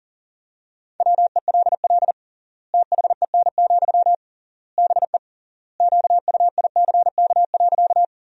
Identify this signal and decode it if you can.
Morse code, heard as a keyed tone, about 40 wpm, 700 Hz. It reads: WELL THEN, BE QUICK.
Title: 3x Word-Spacing - Sentences from Top 500 Words - 40wpm